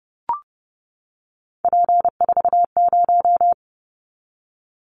Morse code, heard as a keyed tone, about 30 wpm, 700 Hz. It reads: P40
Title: Callsign Prefixes for Top 100 DXCC - 30wpm